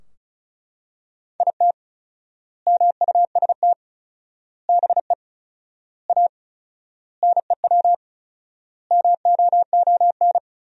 Morse code, heard as a keyed tone, about 34 wpm, 700 Hz. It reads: IT MUST BE A NEW MOON
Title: 4x Word-Spacing - Sentences from Top 500 Words - 35wpm